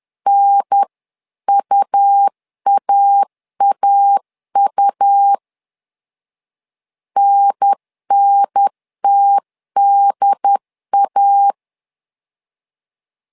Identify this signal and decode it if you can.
Morse code, heard as a keyed tone, about 10 wpm, 800 Hz. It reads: NUAAU NNTDA
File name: LEZIONI / 2 UHDN / 2 UHDN 10wpm 1904